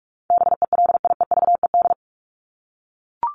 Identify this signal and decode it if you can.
Morse code, heard as a keyed tone, about 44 wpm, 700 Hz. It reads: BELIEVED
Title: ED Suffix - Top 100 Words with Base Words 45wpm